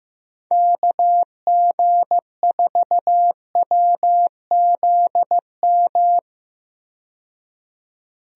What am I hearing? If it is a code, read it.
Morse code, heard as a keyed tone, about 15 wpm, 700 Hz. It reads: KG4WZM
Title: USA Call Signs - 6 Characters - 15wpm